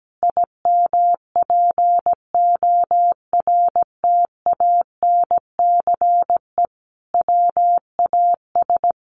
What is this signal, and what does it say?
Morse code, heard as a keyed tone, about 17 wpm, 700 Hz. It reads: IMPORTANCE WAS
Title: Sentences from Top 1400 Words 17wpm